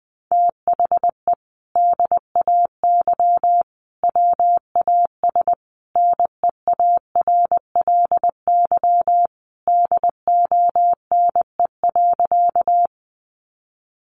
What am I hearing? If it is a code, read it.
Morse code, heard as a keyed tone, about 20 wpm, 700 Hz. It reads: THE DAY WAS NEARLY DONE.
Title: Sentences from Top 1300 Words 20wpm